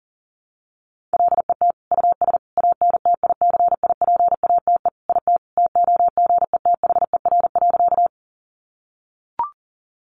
Morse code, heard as a keyed tone, about 37 wpm, 700 Hz. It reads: LET US ANTICIPATE IT TOGETHER.
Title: Sentences from Top 1500 Words 40wpm